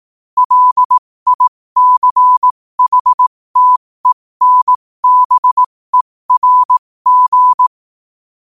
Morse code, heard as a keyed tone, about 18 wpm, spaced as for 15 wpm, 1000 Hz. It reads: LICHTENBERG
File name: qotdfiles / 15wpm